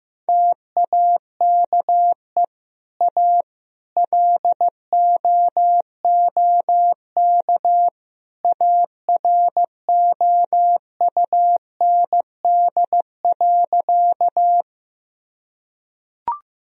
Morse code, heard as a keyed tone, about 15 wpm, 700 Hz. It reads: TAKE A LOOK AROUND.
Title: Sentences from Top 1000 Words - Review Part 3 - 15wpm